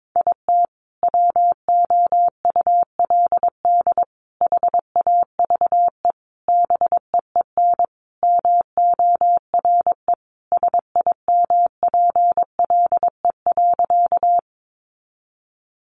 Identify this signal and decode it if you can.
Morse code, heard as a keyed tone, about 22 wpm, 700 Hz. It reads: IT WOULD HAVE BEEN MORE SIMPLE.
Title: Sentences from Top 400 Words 22wpm